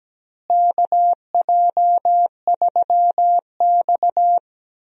Morse code, heard as a keyed tone, about 17 wpm, 700 Hz. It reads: KJ3X
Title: USA Call Signs - 4 Characters - Repeated 2x - 17wpm